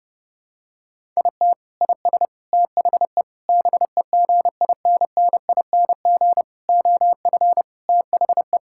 Morse code, heard as a keyed tone, about 32 wpm, 700 Hz. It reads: IT IS THE BEGINNING OF THE
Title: Sentences from Top 1000 Words - Review Part 5 - 30wpm